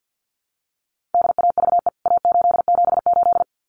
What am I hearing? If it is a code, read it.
Morse code, heard as a keyed tone, about 50 wpm, 700 Hz. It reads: DAVE A878
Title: CWT Exchange 50wpm